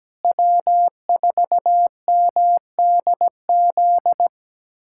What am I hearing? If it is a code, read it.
Morse code, heard as a keyed tone, about 17 wpm, 700 Hz. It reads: W4MDZ
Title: USA Callsign DE Callsign - 5 Characters - 17wpm